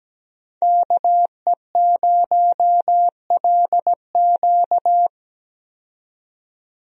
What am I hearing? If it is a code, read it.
Morse code, heard as a keyed tone, about 17 wpm, 700 Hz. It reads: KE0LQ
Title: USA Call Signs - 5 Characters - 17wpm